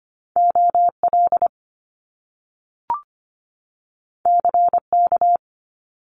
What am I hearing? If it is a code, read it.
Morse code, heard as a keyed tone, about 25 wpm, 700 Hz. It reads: OL CK
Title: Top Two Letter Combinations in Words 25wpm